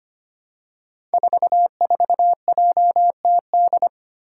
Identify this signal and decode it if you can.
Morse code, heard as a keyed tone, about 25 wpm, 700 Hz. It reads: 44JTD
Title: Five Challenging Letter-Number Combinations 25wpm